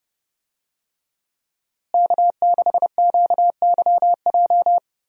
Morse code, heard as a keyed tone, about 32 wpm, 700 Hz. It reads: K6QYJ
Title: USA Call Signs - 5 Characters - Repeated 3x - 30wpm